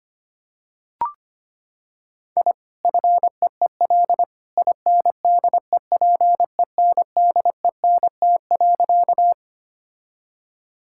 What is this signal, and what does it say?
Morse code, heard as a keyed tone, about 25 wpm, 700 Hz. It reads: I FEEL INDEPENDENT.
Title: Sentences from Top 1000 Words - Review Part 6 - 25wpm